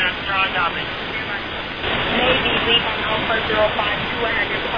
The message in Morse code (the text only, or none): none